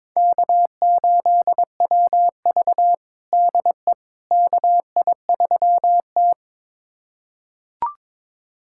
Morse code, heard as a keyed tone, about 21 wpm, 700 Hz. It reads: K8WV DE KI3T
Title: USA Callsign DE Callsign - 4 Characters - 22wpm